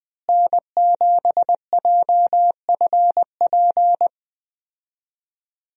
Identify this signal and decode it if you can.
Morse code, heard as a keyed tone, about 20 wpm, 700 Hz. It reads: N7JFP
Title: USA Call Signs - 5 Characters - Repeated 3x - 20wpm